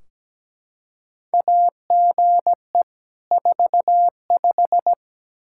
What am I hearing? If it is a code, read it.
Morse code, heard as a keyed tone, about 17 wpm, 700 Hz. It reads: AGE 45
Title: Practice QSO - Exchange 3 - More Info - 17wpm